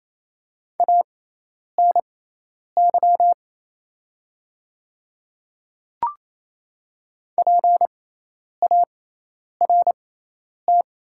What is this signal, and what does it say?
Morse code, heard as a keyed tone, about 28 wpm, spaced as for 10 wpm, 700 Hz. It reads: ANY PART
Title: 6x Character Spacing - Top 200 Words - 28wpm